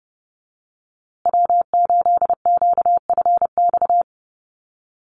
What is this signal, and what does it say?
Morse code, heard as a keyed tone, about 30 wpm, 700 Hz. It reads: W8QFX
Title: USA Call Signs Repeated 2x 30wpm